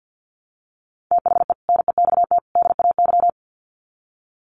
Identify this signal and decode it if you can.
Morse code, heard as a keyed tone, about 50 wpm, 700 Hz. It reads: THE NEXT DAY
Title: Sets of 3 Words 50wpm